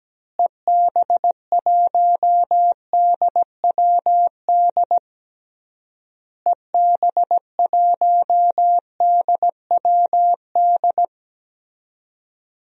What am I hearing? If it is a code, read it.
Morse code, heard as a keyed tone, about 17 wpm, 700 Hz. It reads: EB1DWD EB1DWD
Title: DXCC-100 Call Signs - 6 Characters - Repeated 3x - 17wpm